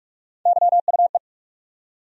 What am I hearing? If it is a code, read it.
Morse code, heard as a keyed tone, about 42 wpm, 700 Hz. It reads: YUE
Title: USA Call Sign Postfixes - 3 Letters 45wpm